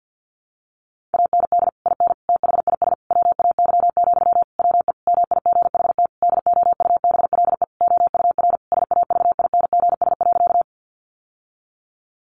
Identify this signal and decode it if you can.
Morse code, heard as a keyed tone, about 50 wpm, 700 Hz. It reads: AND IN THIS WAY, WE MIGHT DOUBLE OUR SAVINGS.
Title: Sentences from Top 1400 Words 50wpm